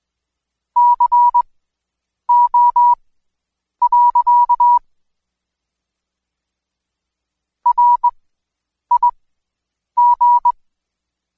Morse code, heard as a keyed tone, about 20 wpm, spaced as for 8 wpm, 970 Hz. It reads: CO. RIG
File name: Code Quick / AudioDownld2 / Seg114-Day19-DayDrill 19